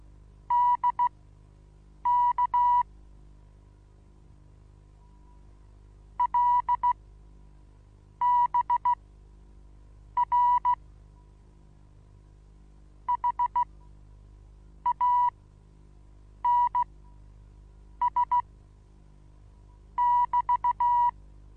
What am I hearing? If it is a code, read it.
Morse code, heard as a keyed tone, about 16 wpm, spaced as for 6 wpm, 970 Hz. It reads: DK LBR HANS <BT>
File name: Onlinelehrgang / morsen / DJ4UF Morsen25